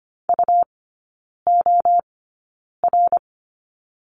Morse code, heard as a keyed tone, about 25 wpm, spaced as for 9 wpm, 700 Hz. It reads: UOR
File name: course / 25:10wpm / 100 Lesson 17e - Letter u - Three Characters 25wpm